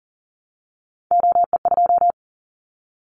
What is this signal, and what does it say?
Morse code, heard as a keyed tone, about 40 wpm, 700 Hz. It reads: OE2
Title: Callsign Prefixes for Top 25 DXCC - Repeated 3x - 40wpm